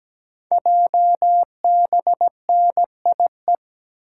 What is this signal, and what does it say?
Morse code, heard as a keyed tone, about 17 wpm, 700 Hz. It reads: JBNIE
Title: Code Groups - 5 Letters - 17wpm